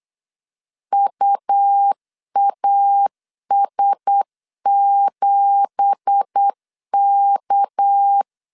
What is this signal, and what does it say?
Morse code, heard as a keyed tone, about 8 wpm, 800 Hz. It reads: UAS7K
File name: CW / DET / 1 9 1500C S B 8wpm 5624